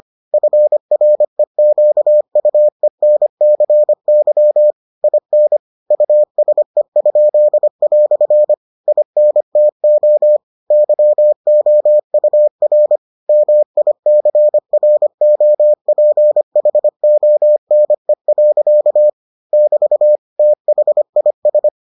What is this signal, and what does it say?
Morse code, heard as a keyed tone, about 25 wpm, 600 Hz. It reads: FREQUENCY IN USE?" INTO YOUR MICROPHONE. <BT> THIS